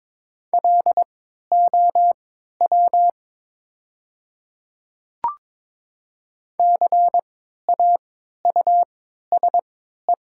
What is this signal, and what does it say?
Morse code, heard as a keyed tone, about 22 wpm, spaced as for 12 wpm, 690 Hz. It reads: LOW CAUSE
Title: 3x Character Spacing - Top 200 Words - 22wpm